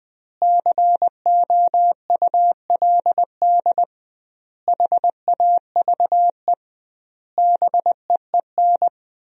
Morse code, heard as a keyed tone, about 20 wpm, 700 Hz. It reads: COULD HAVE BEEN
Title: 2x Word-Spacing - Sentences from Top 500 Words - 20wpm